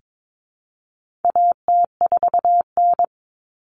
Morse code, heard as a keyed tone, about 22 wpm, 700 Hz. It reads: AT4N
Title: Common Cut Numbers - 4-Digit 22wpm